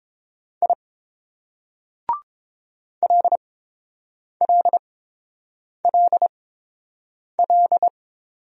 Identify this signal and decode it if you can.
Morse code, heard as a keyed tone, about 30 wpm, spaced as for 7 wpm, 700 Hz. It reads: I LLLL